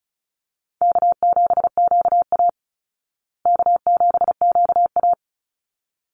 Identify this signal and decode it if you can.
Morse code, heard as a keyed tone, about 34 wpm, 700 Hz. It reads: K7QA K7QA
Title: Easy Call Signs Repeated 2x - 35wpm